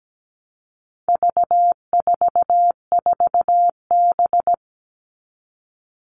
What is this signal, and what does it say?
Morse code, heard as a keyed tone, about 17 wpm, 700 Hz. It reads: V44B
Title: All Cut Numbers - 4-Digit 17wpm